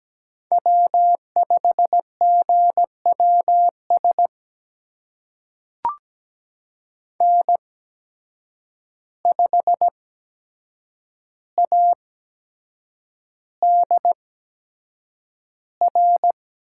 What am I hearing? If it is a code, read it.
Morse code, heard as a keyed tone, about 17 wpm, spaced as for 7 wpm, 700 Hz. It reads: W5GWS N5ADR
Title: USA Call Signs - 5 Characters - 8x Character Spacing - 17wpm